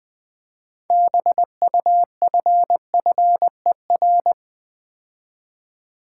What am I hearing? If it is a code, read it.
Morse code, heard as a keyed tone, about 20 wpm, 700 Hz. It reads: BUFFER
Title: Build A Word - 2 Letters At A Time - 6-Letter Words - 20wpm